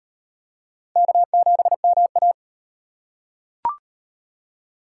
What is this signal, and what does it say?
Morse code, heard as a keyed tone, about 33 wpm, 700 Hz. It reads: K7MA